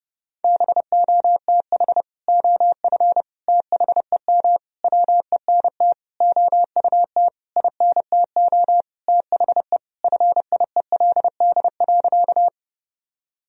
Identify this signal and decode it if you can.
Morse code, heard as a keyed tone, about 32 wpm, 700 Hz. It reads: BOTH OF THEM WENT OUT INTO THE FIELD.
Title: Sentences from Top 1000 Words - Review Part 2 - 30wpm